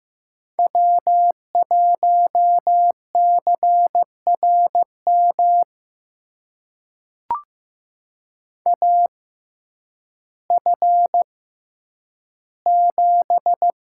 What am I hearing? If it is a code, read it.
Morse code, heard as a keyed tone, about 15 wpm, spaced as for 9 wpm, 700 Hz. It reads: W1CRM AF7
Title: USA Call Signs - 5 Characters - 6x Character Spacing - 15wpm